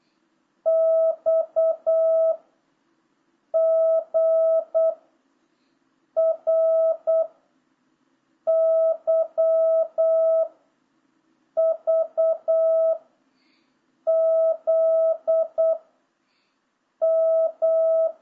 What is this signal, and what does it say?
Morse code, heard as a keyed tone, about 8 wpm, 650 Hz. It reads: X G R Y V Z M